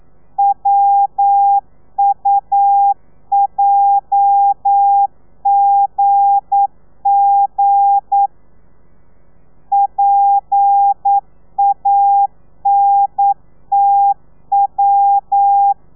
Morse code, heard as a keyed tone, about 9 wpm, 800 Hz. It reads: WUJGG PANTW
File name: telegrafia / i2tfj / Lez8 40